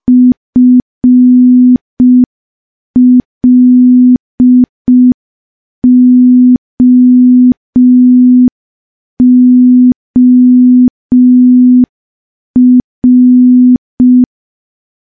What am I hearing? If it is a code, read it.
Morse code, heard as a keyed tone, about 5 wpm, 250 Hz. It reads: FLOOR
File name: morse / morse-training-05WPM